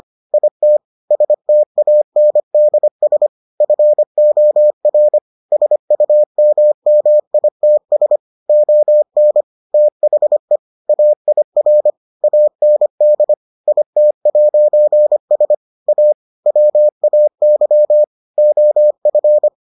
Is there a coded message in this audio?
Yes: Morse code from 0.3 to 19.5 s, heard as a keyed tone, about 25 wpm, 600 Hz. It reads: IT STANDS FOR SUMMITS ON THE AIR AND IT'S A WAY OF